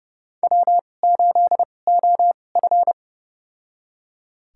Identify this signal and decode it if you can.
Morse code, heard as a keyed tone, about 30 wpm, spaced as for 22 wpm, 700 Hz. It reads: W8OF
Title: USA Call Signs - 4 Characters - 2x Character Spacing - 30wpm